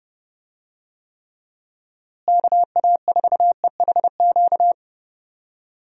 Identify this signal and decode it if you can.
Morse code, heard as a keyed tone, about 32 wpm, 700 Hz. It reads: KA4EHQ
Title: USA Call Signs - 6 Characters - Repeated 3x - 30wpm